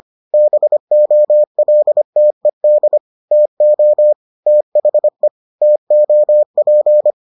Morse code, heard as a keyed tone, about 25 wpm, 600 Hz. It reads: BOLTED TO THE TOP